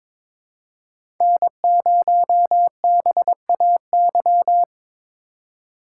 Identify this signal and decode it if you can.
Morse code, heard as a keyed tone, about 23 wpm, 700 Hz. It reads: N0BAY
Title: USA Call Signs - 5 Characters - 3x Character Spacing - 22wpm